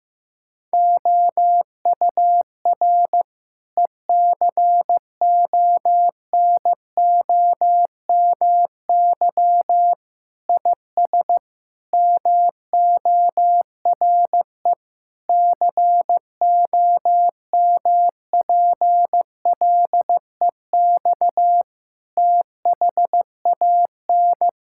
Morse code, heard as a keyed tone, about 15 wpm, 700 Hz. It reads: OUR ECONOMY IS MORE COMPLEX THAN
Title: Sentences from Top 1000 Words - Review Part 6 - 15wpm